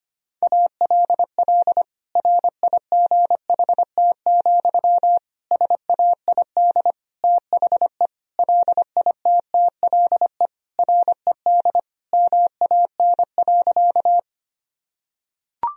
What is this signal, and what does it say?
Morse code, heard as a keyed tone, about 25 wpm, 700 Hz. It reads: ALL RIGHT, SAID THE LITTLE RED MAN.
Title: Sentences from Top 1000 Words - Review Part 2 - 25wpm